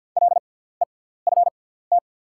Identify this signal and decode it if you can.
Morse code, heard as a keyed tone, about 48 wpm, spaced as for 17 wpm, 690 Hz. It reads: LEFT
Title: 6x Character Spacing - Top 300 Words - 50wpm